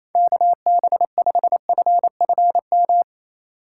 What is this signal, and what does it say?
Morse code, heard as a keyed tone, about 28 wpm, 700 Hz. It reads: KB5FFM
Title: USA Call Signs - 4x Character Spacing - 28wpm